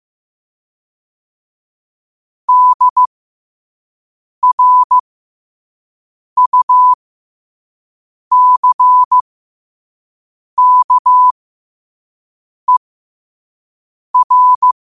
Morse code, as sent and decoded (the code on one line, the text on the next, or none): -.. .-. ..- -.-. -.- . .-.
DRUCKER